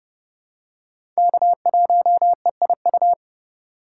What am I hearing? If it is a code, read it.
Morse code, heard as a keyed tone, about 30 wpm, 700 Hz. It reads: K1EIU
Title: USA Call Signs - 2x Character Spacing - 30wpm